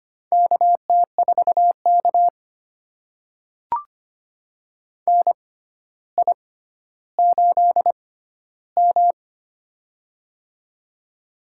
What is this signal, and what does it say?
Morse code, heard as a keyed tone, about 25 wpm, spaced as for 13 wpm, 700 Hz. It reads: KT4K NI8M
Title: USA Call Signs - 4 Characters - 6x Character Spacing - 25wpm